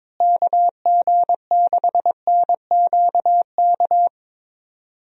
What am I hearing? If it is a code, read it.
Morse code, heard as a keyed tone, about 22 wpm, 700 Hz. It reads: KG6NQK
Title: USA Call Signs - 6 Characters - Repeated 2x - 22wpm